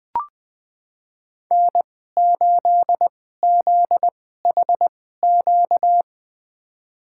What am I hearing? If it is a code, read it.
Morse code, heard as a keyed tone, about 20 wpm, spaced as for 14 wpm, 700 Hz. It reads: N8ZHQ
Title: USA Call Signs - 5 Characters - 2x Character Spacing - 20wpm